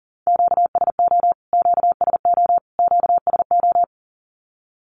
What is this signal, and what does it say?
Morse code, heard as a keyed tone, about 40 wpm, 700 Hz. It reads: QSO QSO QSO
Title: Q-Codes Repeated 3x 40wpm